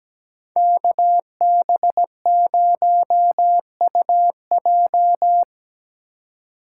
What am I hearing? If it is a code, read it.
Morse code, heard as a keyed tone, about 20 wpm, 700 Hz. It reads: KB0UJ
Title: Speed Racing - USA Call Signs 17wpm